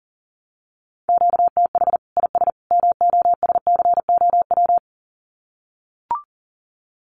Morse code, heard as a keyed tone, about 40 wpm, 700 Hz. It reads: QTH IS MOSCOW
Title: Common QSO Phrases 40wpm